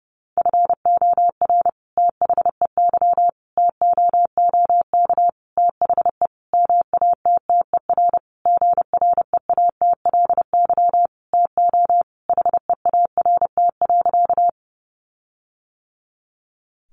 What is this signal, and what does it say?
Morse code, heard as a keyed tone, about 30 wpm, 700 Hz. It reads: FOR THEY TOOK THE MATTER GREATLY TO HEART.
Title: Sentences from Top 1400 Words 30wpm